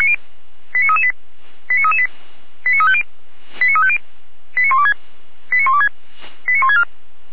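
Five-tone selective call (ZVEI), heard as quick runs of five tones, tones of about 70 ms. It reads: 89338 89338 89350 89350 89126 89126 89164
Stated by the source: ZVEI five-tone call received by SDR.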